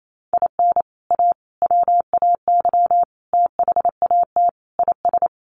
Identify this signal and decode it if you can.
Morse code, heard as a keyed tone, about 28 wpm, 700 Hz. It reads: IN A WAY THAT IS
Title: Sets of 5 Words Encore 28wpm